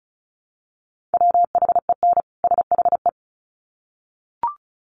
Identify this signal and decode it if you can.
Morse code, heard as a keyed tone, about 34 wpm, 700 Hz. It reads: WHEN SHE
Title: Sets of 2 Words 35wpm